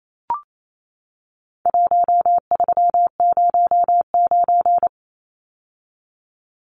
Morse code, heard as a keyed tone, about 28 wpm, 700 Hz. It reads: 1309
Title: Numbers - 4-Digit 28wpm